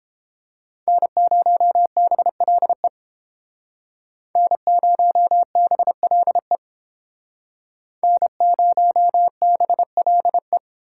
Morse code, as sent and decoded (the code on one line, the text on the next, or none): -. ----- -... .-.. / -. ----- -... .-.. / -. ----- -... .-.. .
N0BLE N0BLE N0BLE